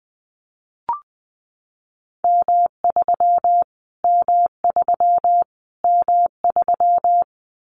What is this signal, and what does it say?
Morse code, heard as a keyed tone, about 20 wpm, 700 Hz. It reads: M3 M3 M3
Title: Callsign Prefixes for Top 25 DXCC - Repeated 3x - 20wpm